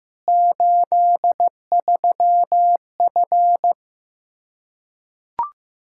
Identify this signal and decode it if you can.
Morse code, heard as a keyed tone, about 16 wpm, 700 Hz. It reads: 83F
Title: Practice QSO - Exchange 2 - Info - 15wpm